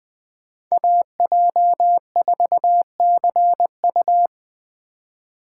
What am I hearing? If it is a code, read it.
Morse code, heard as a keyed tone, about 21 wpm, 700 Hz. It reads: AJ4CU